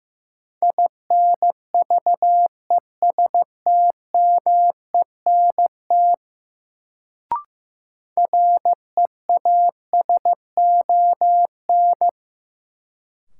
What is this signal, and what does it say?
Morse code, heard as a keyed tone, about 15 wpm, 700 Hz. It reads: INVESTMENT REASON
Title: Sentences from Top 700 Words - New Word Drill - 15wpm